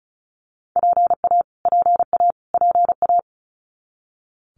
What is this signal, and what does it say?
Morse code, heard as a keyed tone, about 34 wpm, 700 Hz. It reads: PA PA PA
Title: US States Repeated 3x 35wpm